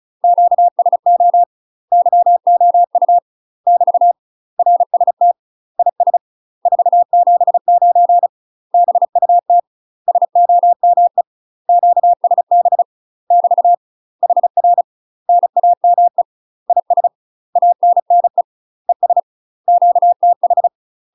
Morse code, heard as a keyed tone, about 34 wpm, 700 Hz. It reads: QSO YOU <BT> RST IS 479 BUT SOME QSB <BT> HR NAME IS ANNE ES QTH
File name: UK MASTER QSOs TRAINING / 35WPM / UK MASTER QSOs 0001